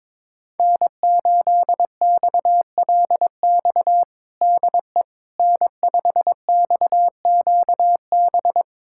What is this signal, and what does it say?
Morse code, heard as a keyed tone, about 22 wpm, 700 Hz. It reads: N8XLX DE N5XQB
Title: USA Callsign DE Callsign - 5 Characters - 22wpm